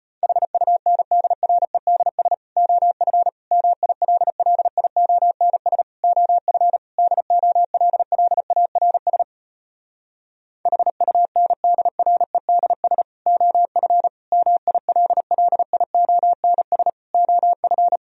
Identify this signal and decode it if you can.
Morse code, heard as a keyed tone, about 33 wpm, 700 Hz. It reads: HUNDREDS OF MILLIONS OF DOLLARS HUNDREDS OF MILLIONS OF